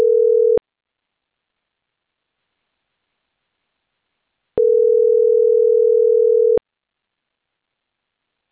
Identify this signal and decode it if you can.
Ringback tone (North America), 440 + 480 Hz.